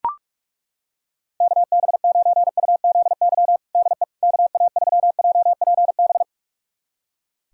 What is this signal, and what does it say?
Morse code, heard as a keyed tone, about 43 wpm, 700 Hz. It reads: KB9UZY DE KA3JPB
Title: USA Callsign DE Callsign - 6 Characters - 45wpm